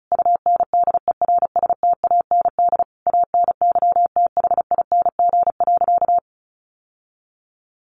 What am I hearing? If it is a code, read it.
Morse code, heard as a keyed tone, about 34 wpm, 700 Hz. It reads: UNDERSTAND ANYTHING.
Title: Sentences from Top 1000 Words 35wpm